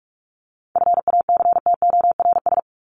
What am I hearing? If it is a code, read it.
Morse code, heard as a keyed tone, about 44 wpm, 700 Hz. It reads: FACTORS
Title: ES:S Suffix - Top 100 Words with Base Words 45wpm